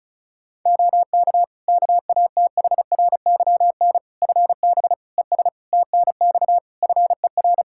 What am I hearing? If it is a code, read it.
Morse code, heard as a keyed tone, about 34 wpm, 700 Hz. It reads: OK KATHRYN FB ES TNX FER
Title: Practice QSO - Exchange 2 - Info - 35wpm